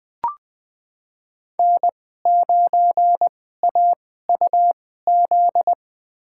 Morse code, heard as a keyed tone, about 21 wpm, spaced as for 15 wpm, 700 Hz. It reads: N9AUZ
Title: USA Call Signs - 2x Character Spacing - 20wpm